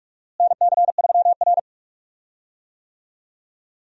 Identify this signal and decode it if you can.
Morse code, heard as a keyed tone, about 46 wpm, 700 Hz. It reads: NC3R
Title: USA Call Signs - 4 Characters - Repeated 2x - 45wpm